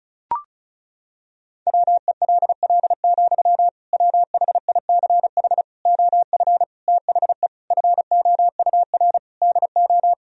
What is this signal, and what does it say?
Morse code, heard as a keyed tone, about 36 wpm, 700 Hz. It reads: WELL, WHICH OF THE FOUR DO
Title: Sentences from Top 1000 Words - Review Part 1 - 35wpm